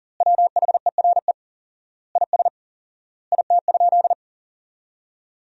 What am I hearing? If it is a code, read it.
Morse code, heard as a keyed tone, about 40 wpm, 700 Hz. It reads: WHERE IS IT?
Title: 4x Word-Spacing - Sentences from Top 200 Words - 40wpm